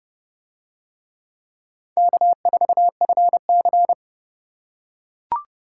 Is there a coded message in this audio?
Yes: Morse code from 2.0 to 3.9 s, heard as a keyed tone, about 29 wpm, 700 Hz. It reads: K4FC